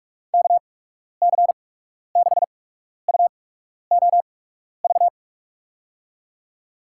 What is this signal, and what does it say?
Morse code, heard as a keyed tone, about 48 wpm, spaced as for 12 wpm, 700 Hz. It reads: KC6UOV
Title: USA Call Signs - 6 Characters - 8x Character Spacing - 45wpm